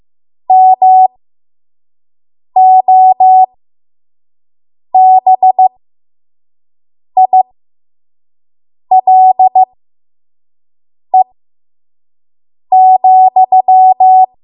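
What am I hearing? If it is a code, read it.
Morse code, heard as a keyed tone, about 15 wpm, spaced as for 5 wpm, 750 Hz. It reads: MOBILE,
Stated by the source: 5 WPM Morse code practice.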